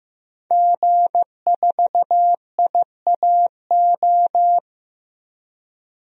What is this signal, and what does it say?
Morse code, heard as a keyed tone, about 15 wpm, 700 Hz. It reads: G4IAO
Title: DXCC-100 Call Signs - 5 Characters - Repeated 2x - 15wpm